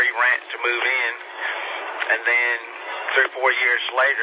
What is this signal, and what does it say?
Emergency Alert System (SAME) tones, heard as harsh buzzing bursts, with speech. It is heard from 0.8 to 3.8 s.